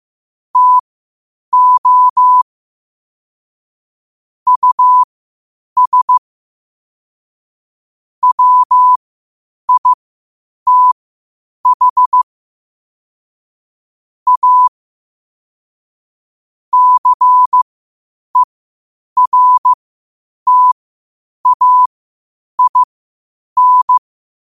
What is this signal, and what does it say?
Morse code, heard as a keyed tone, about 15 wpm, spaced as for 8 wpm, 1000 Hz. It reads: TO US WITH A CERTAIN
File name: qotdfiles / 8wpm